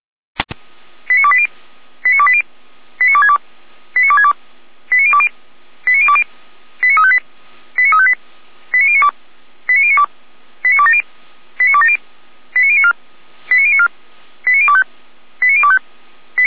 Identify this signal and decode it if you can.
Five-tone selective call (ZVEI), heard as quick runs of five tones, tones of about 70 ms. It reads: 89280 89280 89262 89262 89020 89020 89358 89358 89092 89092 89270 89270 89094 89094 89025 89025
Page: SDR/ZVEI